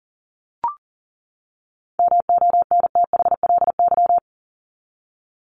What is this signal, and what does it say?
Morse code, heard as a keyed tone, about 40 wpm, 700 Hz. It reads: MONTHLY
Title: LY Suffix - Top 100 Words with Base Words 40wpm